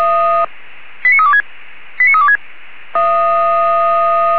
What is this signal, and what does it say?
Five-tone selective call (ZVEI), heard as quick runs of five tones, tones of about 80 ms. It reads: 89326 89326